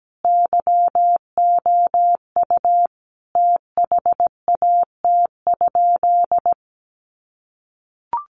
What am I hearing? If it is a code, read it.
Morse code, heard as a keyed tone, about 17 wpm, 700 Hz. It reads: YOU THAT?
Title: Sentences from Top 400 Words 17wpm